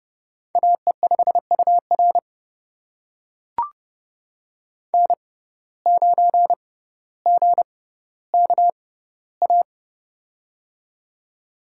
Morse code, heard as a keyed tone, about 30 wpm, spaced as for 15 wpm, 700 Hz. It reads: AE5UR N9GKA
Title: USA Call Signs - 5 Characters - 6x Character Spacing - 30wpm